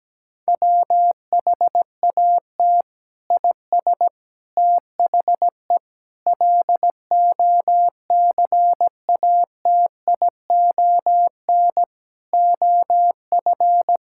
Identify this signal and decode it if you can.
Morse code, heard as a keyed tone, about 17 wpm, 700 Hz. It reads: WHAT IS THE LOCATION OF
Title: Sentences from Top 1000 Words - Review Part 6 - 17wpm